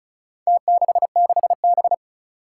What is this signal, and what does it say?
Morse code, heard as a keyed tone, about 36 wpm, 700 Hz. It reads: T66B
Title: Four Challenging Letter-Number Combinations 35wpm